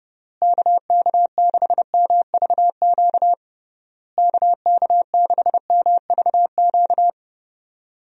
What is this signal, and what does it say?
Morse code, heard as a keyed tone, about 32 wpm, 700 Hz. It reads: KK6MVQ KK6MVQ
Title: USA Call Signs - 6 Characters - Repeated 3x - 30wpm